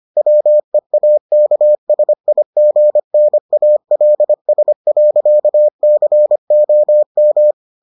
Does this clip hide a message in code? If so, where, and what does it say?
Yes: Morse code from 0.2 to 7.5 s, heard as a keyed tone, about 25 wpm, 600 Hz. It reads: WEAKSIGNALS.COM